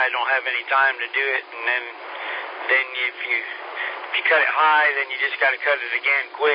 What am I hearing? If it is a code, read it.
Emergency Alert System (SAME) tones (header bursts), heard as harsh buzzing bursts, with speech. It is heard from 0.2 to 5.6 s.